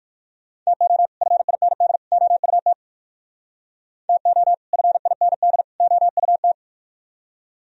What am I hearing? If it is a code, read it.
Morse code, heard as a keyed tone, about 57 wpm, 700 Hz. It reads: TO FIND OUT TO FIND OUT